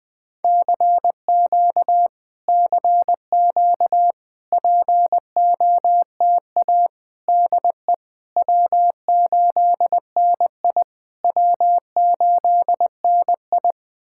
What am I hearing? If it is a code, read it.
Morse code, heard as a keyed tone, about 20 wpm, 690 Hz. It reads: CQ CQ POTA DE W8NI W8NI
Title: POTA QSO Practice - 20wpm